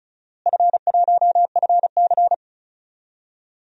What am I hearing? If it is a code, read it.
Morse code, heard as a keyed tone, about 36 wpm, 700 Hz. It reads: F1FC
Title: DXCC-100 Call Signs - 4 Characters - Repeated 2x - 35wpm